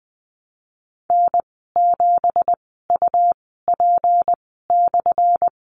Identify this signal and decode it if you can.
Morse code, heard as a keyed tone, about 20 wpm, 700 Hz. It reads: N 7 U P /
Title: 179 Lesson 29g - Slash - Call Signs 20wpm